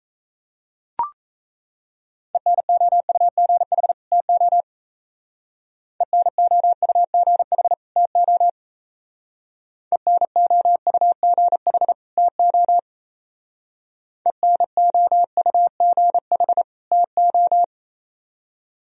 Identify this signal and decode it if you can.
Morse code, heard as a keyed tone, about 33 wpm, 700 Hz. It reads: ENOUGH TO ENOUGH TO ENOUGH TO ENOUGH TO
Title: Speed-Racing - Sets of 2 Words 28wpm